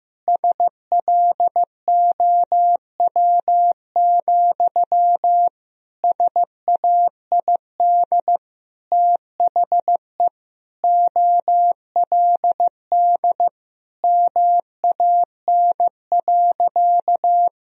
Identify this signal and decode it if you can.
Morse code, heard as a keyed tone, about 16 wpm, 700 Hz. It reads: SLOW, SAID THE OLD MAN.
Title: Sentences from Top 1000 Words - Review Part 2 - 15wpm